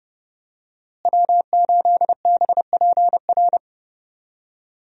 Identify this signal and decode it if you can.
Morse code, heard as a keyed tone, about 30 wpm, 700 Hz. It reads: W8BPR
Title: USA Call Signs - 5 Characters - Repeated 3x - 30wpm